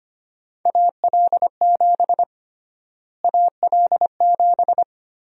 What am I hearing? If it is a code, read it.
Morse code, heard as a keyed tone, about 25 wpm, 700 Hz. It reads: AL7 AL7
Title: USA Call Sign Prefixes - Repeated 2x - 25wpm